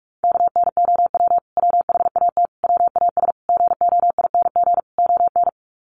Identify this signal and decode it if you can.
Morse code, heard as a keyed tone, about 44 wpm, 700 Hz. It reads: KNOW WHAT WAS GOING ON
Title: Sets of 5 Words Encore2 45wpm